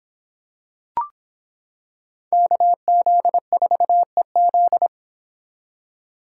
Morse code, heard as a keyed tone, about 25 wpm, 700 Hz. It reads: KZ4EZ